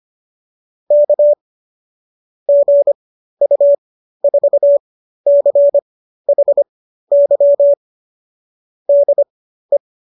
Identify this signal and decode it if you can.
Morse code, heard as a keyed tone, about 25 wpm, spaced as for 13 wpm, 570 Hz. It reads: K GU4CHY DE